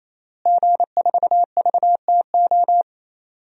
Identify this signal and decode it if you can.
Morse code, heard as a keyed tone, about 27 wpm, 700 Hz. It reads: G4VTO
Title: DXCC-100 Call Signs - 5 Characters - Repeated 3x - 28wpm